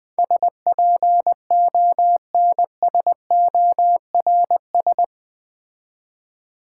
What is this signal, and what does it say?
Morse code, heard as a keyed tone, about 20 wpm, 700 Hz. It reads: SPONSORS
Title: Build A Word - 4 Letters At A Time - 11-Letter Words - 20wpm